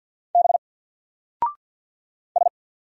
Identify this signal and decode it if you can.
Morse code, heard as a keyed tone, about 48 wpm, 700 Hz. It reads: B S
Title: Single Character (Letter:Number:Punctuation:Prosign) Alt 50wpm